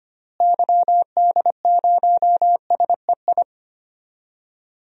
Morse code, heard as a keyed tone, about 25 wpm, 700 Hz. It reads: YD0SEI
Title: DXCC-100 Call Signs - 6 Characters - Repeated 2x - 25wpm